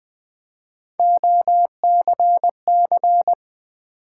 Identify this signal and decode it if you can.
Morse code, heard as a keyed tone, about 20 wpm, 700 Hz. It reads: OCC
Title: Build A Word - 1 Letter At A Time - 3 to 5-Letter Words - 20wpm